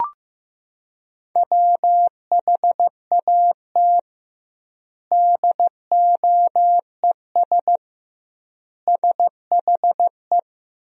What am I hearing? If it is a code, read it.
Morse code, heard as a keyed tone, about 14 wpm, 700 Hz. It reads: WHAT DOES SHE
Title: 2x Word-Spacing - Sentences from Top 200 Words - 15wpm